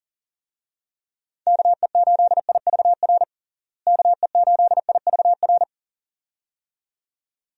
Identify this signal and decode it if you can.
Morse code, heard as a keyed tone, about 40 wpm, 700 Hz. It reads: KE8IVR KE8IVR
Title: USA Call Signs - 6 Characters - Repeated 2x - 40wpm